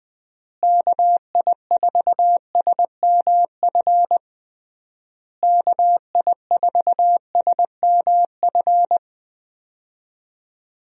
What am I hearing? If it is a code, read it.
Morse code, heard as a keyed tone, about 20 wpm, 700 Hz. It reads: KI4SMF KI4SMF
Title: USA Call Signs - 6 Characters - Repeated 3x - 20wpm